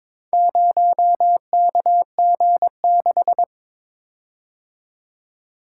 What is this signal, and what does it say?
Morse code, heard as a keyed tone, about 22 wpm, 700 Hz. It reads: 0KG6
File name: practice / 22wpm / Four Letter-Number Combinations - 22wpm